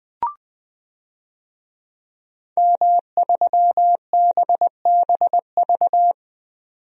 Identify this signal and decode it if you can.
Morse code, heard as a keyed tone, about 20 wpm, 700 Hz. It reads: M3BBV
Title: DXCC-100 Call Signs - 5 Characters - 20wpm